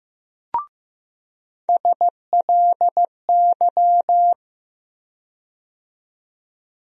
Morse code, heard as a keyed tone, about 15 wpm, 700 Hz. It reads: SLY
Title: USA Call Sign Postfixes - 3 Letters 15wpm